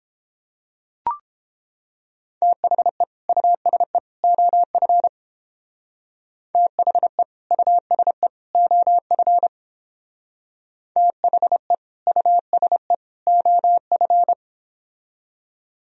Morse code, heard as a keyed tone, about 29 wpm, 700 Hz. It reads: THE USE OF THE USE OF THE USE OF